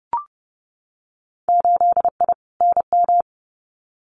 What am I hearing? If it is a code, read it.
Morse code, heard as a keyed tone, about 30 wpm, 700 Hz. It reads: 8I NM
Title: Winter Field Day Exchanges Encore 30wpm